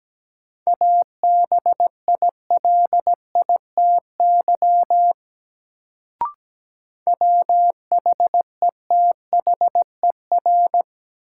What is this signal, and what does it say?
Morse code, heard as a keyed tone, about 17 wpm, 700 Hz. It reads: ABILITY WHETHER
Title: Sets of 3 Words - New Word Drill - 17wpm